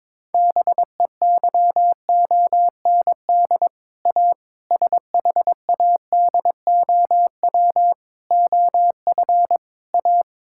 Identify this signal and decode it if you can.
Morse code, heard as a keyed tone, about 22 wpm, 700 Hz. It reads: BEYOND A SHADOW OF A
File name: practice / 22wpm / Sets of 7 Words - 22wpm